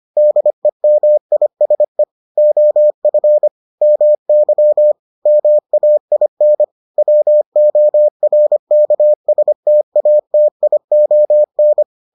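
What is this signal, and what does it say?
Morse code, heard as a keyed tone, about 25 wpm, 600 Hz. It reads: DEMISE OF MY MAIN WORKSTATION
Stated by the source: Morse edition of the Foundations of Amateur Radio episodes.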